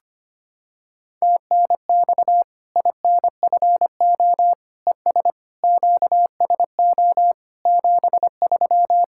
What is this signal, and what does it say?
Morse code, heard as a keyed tone, about 25 wpm, 700 Hz. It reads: TNX INFO ES QSO 73
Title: Practice QSO - Exchange 4 - Ending - 25wpm